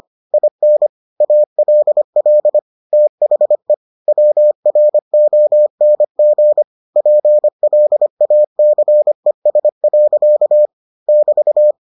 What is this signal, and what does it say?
Morse code, heard as a keyed tone, about 25 wpm, 600 Hz. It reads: IN ALL THE WRONG PLACES. <BT>